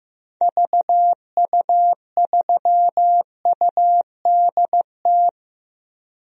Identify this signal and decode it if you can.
Morse code, heard as a keyed tone, about 15 wpm, 700 Hz. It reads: VU3UDT
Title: DXCC-100 Call Signs - 6 Characters - 15wpm